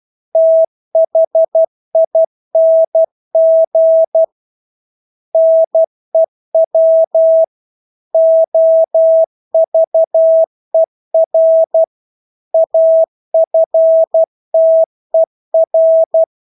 Morse code, heard as a keyed tone, about 12 wpm, 650 Hz. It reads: THING NEW OVER AFTER
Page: Engelske ord 12 wpm